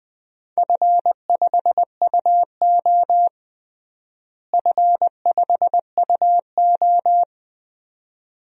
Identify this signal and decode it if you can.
Morse code, heard as a keyed tone, about 21 wpm, 700 Hz. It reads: F5UO F5UO
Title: DXCC-100 Call Signs - 4 Characters - Repeated 2x - 20wpm